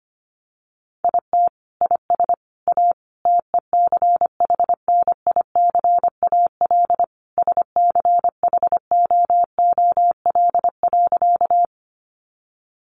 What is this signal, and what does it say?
Morse code, heard as a keyed tone, about 25 wpm, 700 Hz. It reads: IT IS A TECHNICAL SCHOOL.
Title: Sentences from Top 1100 Words 25wpm